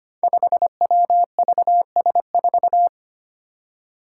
Morse code, heard as a keyed tone, about 25 wpm, 700 Hz. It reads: 5WVS4
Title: Five Challenging Letter-Number Combinations 25wpm